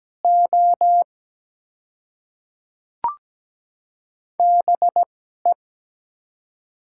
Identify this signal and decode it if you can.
Morse code, heard as a keyed tone, about 17 wpm, spaced as for 12 wpm, 700 Hz. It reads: O BE
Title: 2x Character Spacing - Top 100 Words - 17wpm